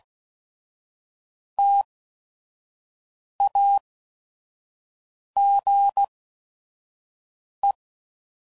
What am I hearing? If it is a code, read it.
Morse code, heard as a keyed tone, about 16 wpm, 800 Hz. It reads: T A G E